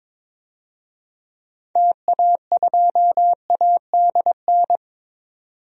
Morse code, heard as a keyed tone, about 22 wpm, 700 Hz. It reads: TA2ADN
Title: DXCC-100 Call Signs - 6 Characters - 22wpm